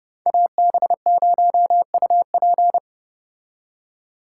Morse code, heard as a keyed tone, about 30 wpm, 700 Hz. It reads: AB0UP